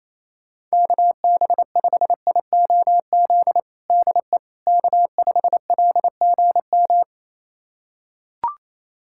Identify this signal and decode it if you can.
Morse code, heard as a keyed tone, about 28 wpm, 700 Hz. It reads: KB5IOZ DE K5LGM
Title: USA Callsign DE Callsign 28wpm